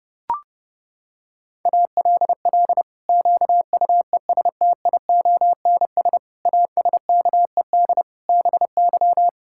Morse code, heard as a keyed tone, about 30 wpm, 700 Hz. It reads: ALL QUESTIONS ASKED BY
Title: Pangram Challenge v2 - 30wpm